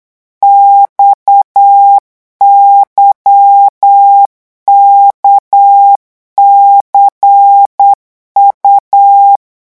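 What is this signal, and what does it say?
Morse code, heard as a keyed tone, about 8 wpm, 800 Hz. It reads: XYKCU
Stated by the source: Morse code revision lesson at 8 words per minute, from a French amateur radio course.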